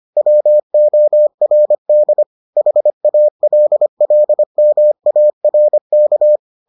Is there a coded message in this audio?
Yes: Morse code from 0.2 to 6.4 s, heard as a keyed tone, about 25 wpm, 600 Hz. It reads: WORD HALLMARK